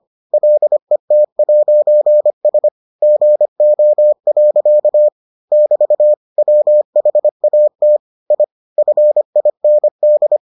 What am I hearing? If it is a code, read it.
Morse code, heard as a keyed tone, about 25 wpm, 600 Hz. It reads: LET'S GO. <BT> WHAT I FIND